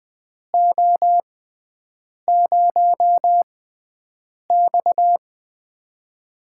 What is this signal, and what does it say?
Morse code, heard as a keyed tone, about 20 wpm, spaced as for 7 wpm, 700 Hz. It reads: O0X
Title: USA Call Signs - 4 Characters - 6x Character Spacing - 20wpm